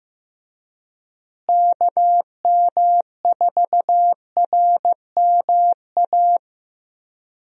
Morse code, heard as a keyed tone, about 15 wpm, 700 Hz. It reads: KM4RMA
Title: USA Call Signs - 6 Characters - Repeated 2x - 15wpm